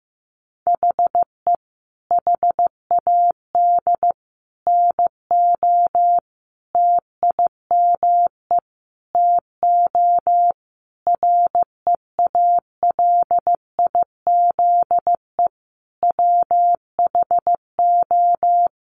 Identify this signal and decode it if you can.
Morse code, heard as a keyed tone, about 15 wpm, 700 Hz. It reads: HE HAD NO TIME TO REALIZE WHO
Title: Sentences from Top 1000 Words 15wpm